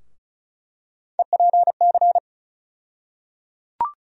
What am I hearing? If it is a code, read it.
Morse code, heard as a keyed tone, about 34 wpm, 700 Hz. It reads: EPC
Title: USA Call Sign Postfixes - 3 Letters 35wpm